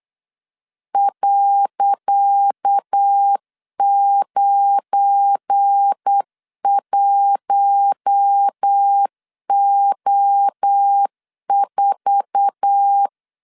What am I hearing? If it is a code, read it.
Morse code, heard as a keyed tone, about 8 wpm, 800 Hz. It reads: .91O4
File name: LEZIONI / 10 SIGNS / 10C DOT 8wpm 3649